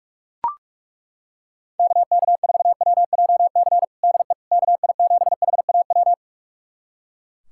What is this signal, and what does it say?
Morse code, heard as a keyed tone, about 41 wpm, 690 Hz. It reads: KK4WJC DE KI7HAW
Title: USA Callsign DE Callsign - 6 Characters - 45wpm